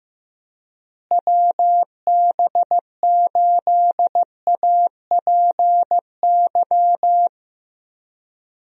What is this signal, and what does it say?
Morse code, heard as a keyed tone, about 15 wpm, 700 Hz. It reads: WB8APY
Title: USA Call Signs - 6 Characters - Repeated 2x - 15wpm